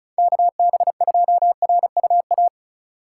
Morse code, heard as a keyed tone, about 34 wpm, 700 Hz. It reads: KB2RUA
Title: USA Call Signs - 6 Characters - 2x Character Spacing - 35wpm